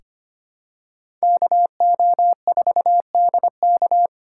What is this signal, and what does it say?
Morse code, heard as a keyed tone, about 25 wpm, 700 Hz. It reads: KO4DK
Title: USA Call Signs - 5 Characters - 2x Character Spacing - 25wpm